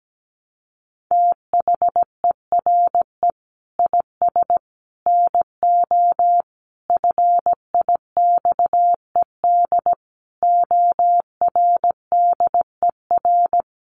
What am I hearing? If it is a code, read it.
Morse code, heard as a keyed tone, about 17 wpm, 700 Hz. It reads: THERE IS NO FIXED ORDER
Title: Sentences from Top 1100 Words 17wpm